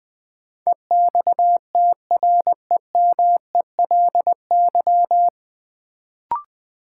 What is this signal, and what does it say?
Morse code, heard as a keyed tone, about 20 wpm, 700 Hz. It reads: EXTREMELY
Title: Sentences from Top 1000 Words - Review Part 7 - New Word Drill - 20wpm